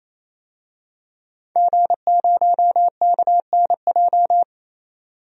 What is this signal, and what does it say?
Morse code, heard as a keyed tone, about 27 wpm, 700 Hz. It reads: G0KNJ
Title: DXCC-100 Call Signs - 5 Characters - 28wpm